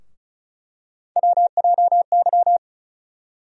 Morse code, heard as a keyed tone, about 34 wpm, 700 Hz. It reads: WJY
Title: USA Call Sign Postfixes - 3 Letters 35wpm